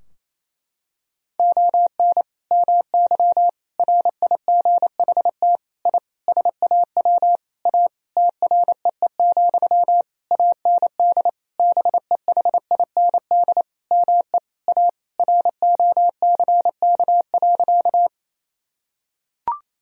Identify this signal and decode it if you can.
Morse code, heard as a keyed tone, about 28 wpm, 700 Hz. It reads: ON MY RIGHT I SAW A TREE, AND BEHIND ME A ROCK.
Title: Sentences from Top 1000 Words - Review Part 2 - 28wpm